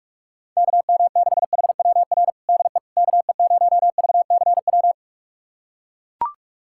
Morse code, heard as a keyed tone, about 41 wpm, 700 Hz. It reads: KM6HWR DE KE0VCW
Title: USA Callsign DE Callsign - 6 Characters - 45wpm